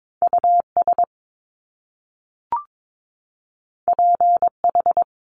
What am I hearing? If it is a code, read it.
Morse code, heard as a keyed tone, about 22 wpm, 700 Hz. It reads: US PH
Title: Top Two Letter Combinations in Words 22wpm